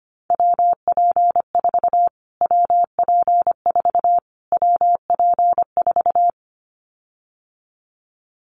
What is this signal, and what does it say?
Morse code, heard as a keyed tone, about 25 wpm, 700 Hz. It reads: WP4 WP4 WP4
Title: Callsign Prefixes for Top 25 DXCC - Repeated 3x - 25wpm